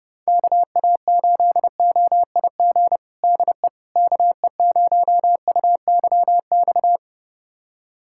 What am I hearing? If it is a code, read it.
Morse code, heard as a keyed tone, about 30 wpm, 700 Hz. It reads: KA8OIG DE KE0UYX
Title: USA Callsign DE Callsign - 6 Characters - 30wpm